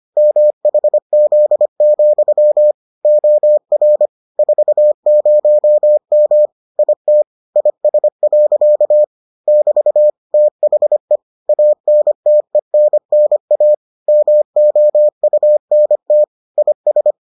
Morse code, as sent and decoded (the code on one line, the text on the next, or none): -- .... --.. --..-- / --- .-. / ....- ----- -- / .. - / .. ... .-.-.- / -...- / - .... / .- -. - . -. -. .- / -- --- ..- -. - / .. ...
MHZ, OR 40M IT IS. <BT> THE ANTENNA MOUNT IS